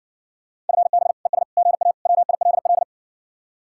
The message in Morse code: FB ES GA WILL

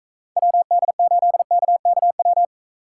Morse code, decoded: WD8KKW

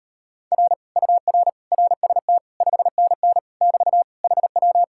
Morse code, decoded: R UR RST 5NN <BT> HW

